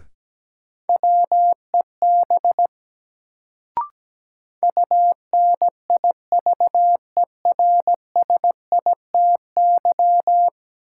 Morse code, heard as a keyed tone, about 17 wpm, 700 Hz. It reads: WEB UNIVERSITY
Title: Sentences from Top 1000 Words - Review Part 4 - New Word Drill - 17wpm